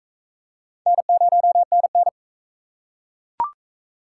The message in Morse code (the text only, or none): N0NN